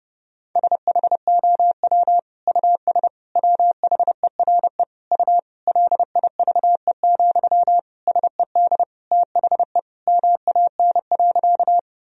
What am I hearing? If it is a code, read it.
Morse code, heard as a keyed tone, about 30 wpm, 700 Hz. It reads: SHOW US WHERE U LIVE, SED THE MAN.
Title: Sentences from Top 200 Words and Abbreviations 30wpm